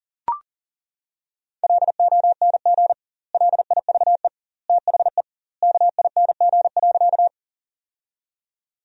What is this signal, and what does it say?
Morse code, heard as a keyed tone, about 40 wpm, 700 Hz. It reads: LONG LIVE THE KING.
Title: 2x Word-Spacing - Sentences from Top 400 Words - 40wpm